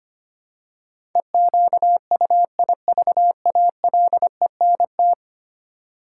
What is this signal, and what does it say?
Morse code, heard as a keyed tone, about 25 wpm, 700 Hz. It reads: EQUIVALENT
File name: 10 Letter Words - First 3 Letters Given - 25wpm